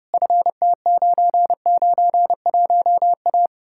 Morse code, heard as a keyed tone, about 40 wpm, 700 Hz. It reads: FT991A